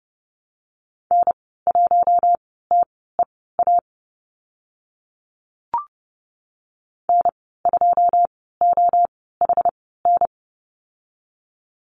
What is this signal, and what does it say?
Morse code, heard as a keyed tone, about 30 wpm, spaced as for 17 wpm, 700 Hz. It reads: N1TEA N2OHN
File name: Lesson 14g - Number 2 - Call Signs 30wpm